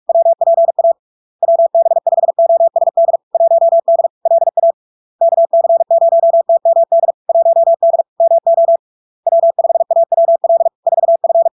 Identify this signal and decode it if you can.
Morse code, heard as a keyed tone, about 46 wpm, 670 Hz. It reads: WPA WB5OSD 1D LA KC0TMD 1D MO W5AWL 4F
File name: LARK-FD-2022-Log-in-CW-45wpm-670-Hz